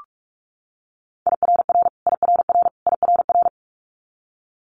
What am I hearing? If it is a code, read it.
Morse code, heard as a keyed tone, about 46 wpm, 700 Hz. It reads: IRR IRR IRR
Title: Multi-letter Phonemes Repeated 3x 45wpm